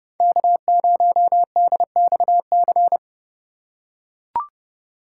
Morse code, heard as a keyed tone, about 30 wpm, 700 Hz. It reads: K0DXC
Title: K1USN SST QSO Practice - 30wpm